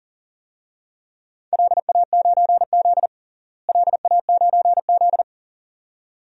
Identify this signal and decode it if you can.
Morse code, heard as a keyed tone, about 40 wpm, 700 Hz. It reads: LA9Z LA9Z
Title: DXCC-100 Call Signs - 4 Characters - Repeated 2x - 40wpm